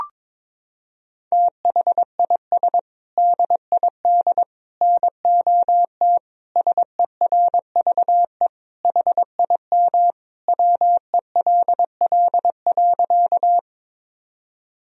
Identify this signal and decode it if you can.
Morse code, heard as a keyed tone, about 22 wpm, 700 Hz. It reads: THIS DID NOT SERVE HIM WELL.
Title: Sentences from Top 1000 Words - Review Part 2 - 22wpm